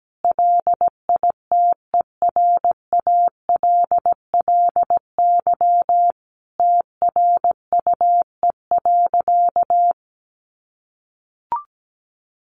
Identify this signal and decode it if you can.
Morse code, heard as a keyed tone, about 17 wpm, 700 Hz. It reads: LITERALLY TRUE.
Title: Sentences from Top 1400 Words 17wpm